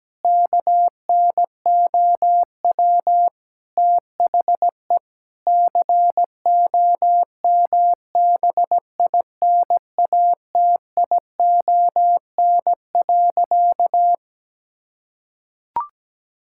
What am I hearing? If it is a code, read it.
Morse code, heard as a keyed tone, about 17 wpm, 700 Hz. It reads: KNOW THE COMBINATION.
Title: Sentences from Top 1000 Words - Review Part 6 - 17wpm